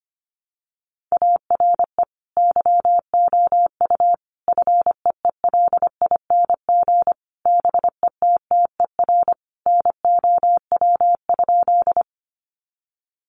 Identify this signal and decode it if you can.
Morse code, heard as a keyed tone, about 25 wpm, 700 Hz. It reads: ARE YOU FEELING BETTER NOW?
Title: Sentences from Top 1200 Words 25wpm